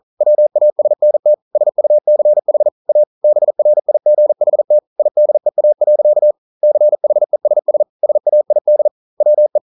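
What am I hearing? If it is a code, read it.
Morse code, heard as a keyed tone, about 40 wpm, 600 Hz. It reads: WASNT SUCH A BRIGHT IDEA. CHESS SAID WE
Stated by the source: Morse code practice at 41 wpm, from CW med Gnister.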